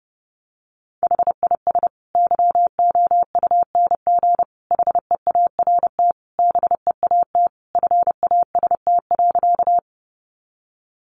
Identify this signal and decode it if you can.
Morse code, heard as a keyed tone, about 28 wpm, 700 Hz. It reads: HIS YOUNG HEART BEAT FAST.
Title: Sentences from Top 1300 Words 30wpm